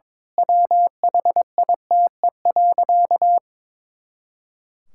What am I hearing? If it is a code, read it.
Morse code, heard as a keyed tone, about 22 wpm, 700 Hz. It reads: WHITE.
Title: Pangram Challenge v2 - 22wpm